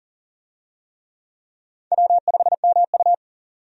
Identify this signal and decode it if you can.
Morse code, heard as a keyed tone, about 40 wpm, 700 Hz. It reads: W5MU
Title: USA Call Signs - 4 Characters - 40wpm